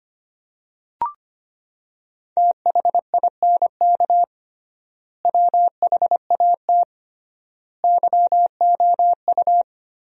Morse code, heard as a keyed tone, about 25 wpm, 700 Hz. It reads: THINK WHAT YOU
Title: 3x Word-Spacing - Sentences from Top 200 Words - 25wpm